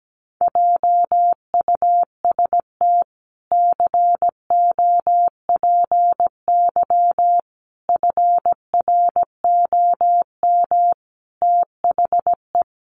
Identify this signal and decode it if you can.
Morse code, heard as a keyed tone, about 17 wpm, 700 Hz. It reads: JUST COPY FROM THE
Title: Sentences from Top 1400 Words 17wpm